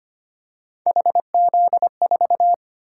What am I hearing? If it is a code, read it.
Morse code, heard as a keyed tone, about 25 wpm, 700 Hz. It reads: HZ4
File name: Three Letter-Number Combinations - 25wpm